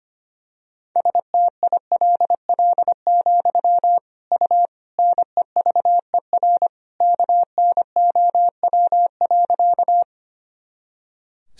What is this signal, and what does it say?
Morse code, heard as a keyed tone, about 25 wpm, 700 Hz. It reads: STILL, U NEVER KNOW.
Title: Sentences from Top 300 Words and Abbreviations 25wpm